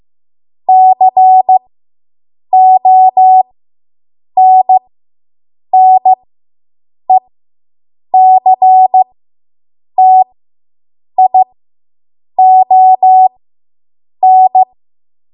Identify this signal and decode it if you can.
Morse code, heard as a keyed tone, about 15 wpm, spaced as for 7 wpm, 750 Hz. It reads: CONNECTION